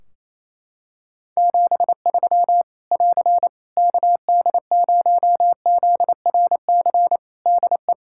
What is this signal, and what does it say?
Morse code, heard as a keyed tone, about 28 wpm, 700 Hz. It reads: 73 <AR> KD0ZRC DE
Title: K1USN SST QSO Practice 28wpm